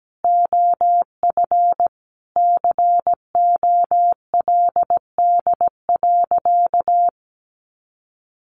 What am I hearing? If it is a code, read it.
Morse code, heard as a keyed tone, about 17 wpm, 700 Hz. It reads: OF COLD.
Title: Sentences from Top 1400 Words 17wpm